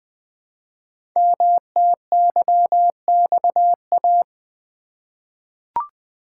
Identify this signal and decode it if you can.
Morse code, heard as a keyed tone, about 20 wpm, 700 Hz. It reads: MTYXA
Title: Code Groups - 5 Letters - 20wpm